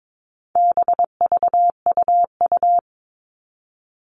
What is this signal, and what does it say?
Morse code, heard as a keyed tone, about 22 wpm, 700 Hz. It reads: BVUU
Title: All Cut Numbers - 4-Digit 22wpm